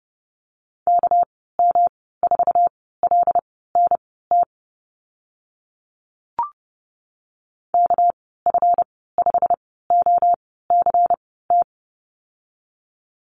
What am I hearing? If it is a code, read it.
Morse code, heard as a keyed tone, about 30 wpm, spaced as for 17 wpm, 700 Hz. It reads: KM4LNT KF5OCT
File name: Lesson 37g - Letter x - Call Signs 30wpm